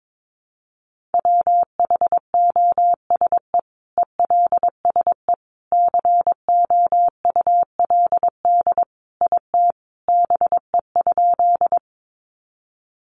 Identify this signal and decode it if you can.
Morse code, heard as a keyed tone, about 22 wpm, 700 Hz. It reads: WHOSE ELSE COULD IT BE?
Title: Sentences from Top 1200 Words 22wpm